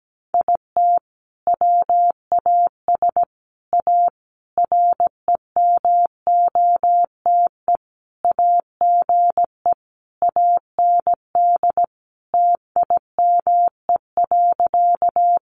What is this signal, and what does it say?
Morse code, heard as a keyed tone, about 17 wpm, 700 Hz. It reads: IT WAS A REMOTE AGE AND TIME.
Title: Sentences from Top 1300 Words 17wpm